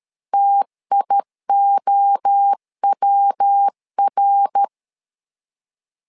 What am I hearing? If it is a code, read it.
Morse code, heard as a keyed tone, about 13 wpm, 800 Hz. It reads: TIOWR
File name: LEZIONI / 3 MWRO / 3 MWRO 12wpm 1730